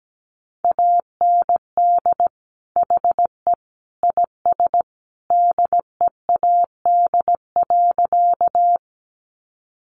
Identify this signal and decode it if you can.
Morse code, heard as a keyed tone, about 17 wpm, 700 Hz. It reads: AND HE IS DEAD.
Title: Sentences from Top 1500 Words 17wpm